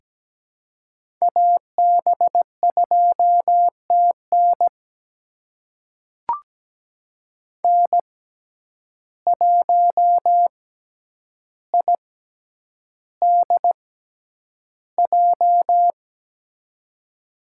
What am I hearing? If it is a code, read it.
Morse code, heard as a keyed tone, about 17 wpm, spaced as for 9 wpm, 700 Hz. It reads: AB2TN N1IDJ
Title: USA Call Signs - 5 Characters - 6x Character Spacing - 17wpm